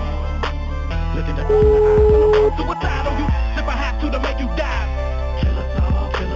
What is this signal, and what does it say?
Ringback tone (Europe, 425 Hz), with noise.